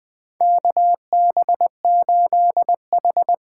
Morse code, heard as a keyed tone, about 20 wpm, 700 Hz. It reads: KB8H
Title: USA Call Signs - 6 Characters - 8x Character Spacing - 20wpm